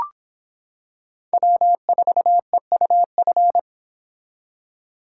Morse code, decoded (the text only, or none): W4EUF